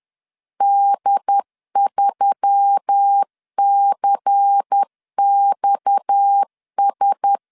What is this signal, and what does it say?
Morse code, heard as a keyed tone, about 10 wpm, 800 Hz. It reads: D3CXS
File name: LEZIONI / 6 PKX0 / 6 PKX 10wpm 2426